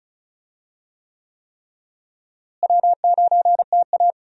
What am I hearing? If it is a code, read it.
Morse code, heard as a keyed tone, about 36 wpm, 700 Hz. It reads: W9TA
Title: USA Call Signs - 4 Characters - Repeated 3x - 35wpm